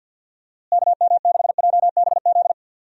Morse code, heard as a keyed tone, about 50 wpm, 700 Hz. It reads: KM6JBZ
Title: USA Call Signs - 6 Characters - Repeated 2x - 50wpm